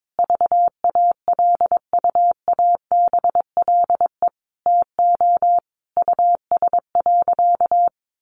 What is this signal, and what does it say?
Morse code, heard as a keyed tone, about 22 wpm, 700 Hz. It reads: VALUABLE TO US.
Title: Sentences from Top 1100 Words 22wpm